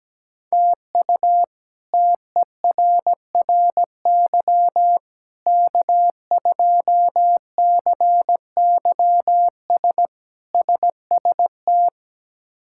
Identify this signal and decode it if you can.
Morse code, heard as a keyed tone, about 17 wpm, 700 Hz. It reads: TU TERRY K2CYS SST
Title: K1USN SST QSO Practice - 17wpm